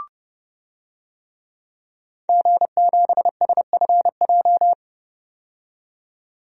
Morse code, heard as a keyed tone, about 31 wpm, 700 Hz. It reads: G7SFJ
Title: DXCC-100 Call Signs - 5 Characters - 30wpm